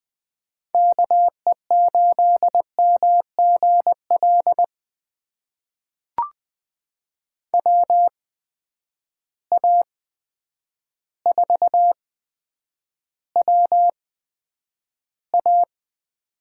Morse code, heard as a keyed tone, about 20 wpm, spaced as for 9 wpm, 700 Hz. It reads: KE8MGL WA4WA
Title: USA Call Signs - 8x Character Spacing - 20wpm